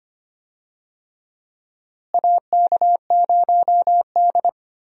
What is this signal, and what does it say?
Morse code, heard as a keyed tone, about 25 wpm, 700 Hz. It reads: AK0D
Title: USA Call Signs - 4 Characters - Repeated 3x - 25wpm